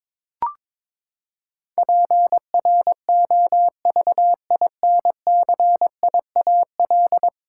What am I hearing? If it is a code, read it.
Morse code, heard as a keyed tone, about 22 wpm, 700 Hz. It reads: PROVINCIAL
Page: morse-code-ninja-practice-22wpm